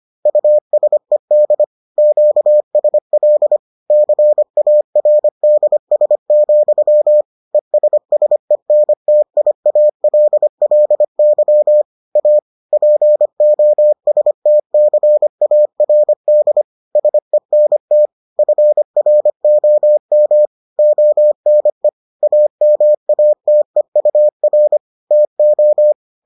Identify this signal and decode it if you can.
Morse code, heard as a keyed tone, about 25 wpm, 600 Hz. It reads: USED QSL CARDS, ESSENTIALLY A POSTCARD SENT FROM ONE AMATEUR TO